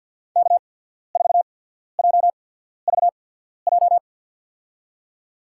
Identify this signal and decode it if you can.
Morse code, heard as a keyed tone, about 53 wpm, spaced as for 14 wpm, 700 Hz. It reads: K4JVJ
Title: USA Call Signs - 5 Characters - 8x Character Spacing - 50wpm